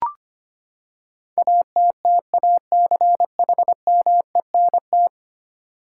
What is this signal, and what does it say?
Morse code, heard as a keyed tone, about 25 wpm, 700 Hz. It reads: ATTACHMENT
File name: Ten Letter Words - 25wpm